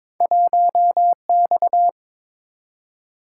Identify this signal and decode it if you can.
Morse code, heard as a keyed tone, about 22 wpm, 700 Hz. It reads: 1X
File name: practice / 22wpm / Two Letter-Number Combinations - 22wpm